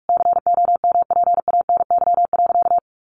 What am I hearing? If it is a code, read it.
Morse code, heard as a keyed tone, about 44 wpm, 700 Hz. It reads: COMPANY.
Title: Sentences from Top 600 Words 45wpm